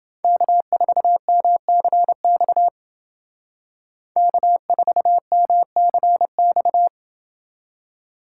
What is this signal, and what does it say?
Morse code, heard as a keyed tone, about 28 wpm, 700 Hz. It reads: K4MCX K4MCX